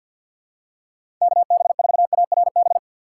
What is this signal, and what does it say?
Morse code, heard as a keyed tone, about 53 wpm, 700 Hz. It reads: KB4ARB
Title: USA Call Signs - 6 Characters - 4x Character Spacing - 50wpm